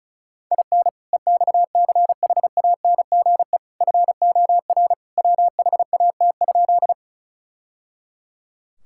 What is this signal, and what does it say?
Morse code, heard as a keyed tone, about 36 wpm, 700 Hz. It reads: IN EXCHANGE FOR WHAT?
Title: Sentences from Top 1000 Words - Review Part 6 - 35wpm